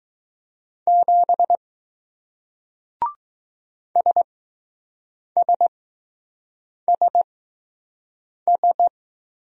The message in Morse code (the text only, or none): 7 SSSS